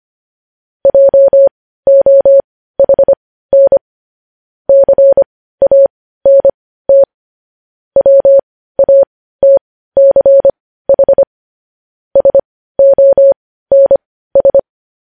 Morse code, heard as a keyed tone, about 25 wpm, spaced as for 15 wpm, 550 Hz. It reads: JOHN CANT WATCH SONS
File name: podcast / News headlines-1x-25x15